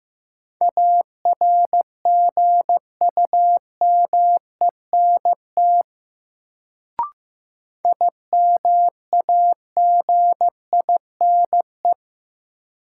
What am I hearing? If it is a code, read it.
Morse code, heard as a keyed tone, about 15 wpm, 700 Hz. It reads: ARGUMENT IMAGINE
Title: Sentences from Top 1400 Words - New Word Drill Alt - 15wpm